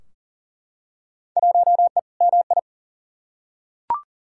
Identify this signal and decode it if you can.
Morse code, heard as a keyed tone, about 40 wpm, 700 Hz. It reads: 1E MI
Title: Field Day Exchanges - 40wpm